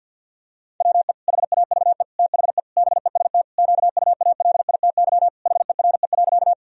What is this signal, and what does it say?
Morse code, heard as a keyed tone, about 52 wpm, 700 Hz. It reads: WE HAVE THE BEST QUALITY HERE.
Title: Sentences from Top 1000 Words - Review Part 4 - 50wpm